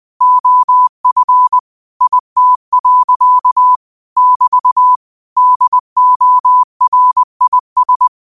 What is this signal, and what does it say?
Morse code, heard as a keyed tone, about 20 wpm, 1000 Hz. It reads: OF IT. <BT> DORIS
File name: 20wpm